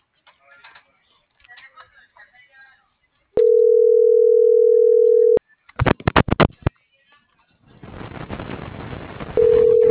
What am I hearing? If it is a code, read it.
Ringback tone (North America), 440 + 480 Hz.